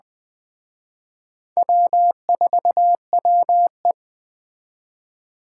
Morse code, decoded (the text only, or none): W4WE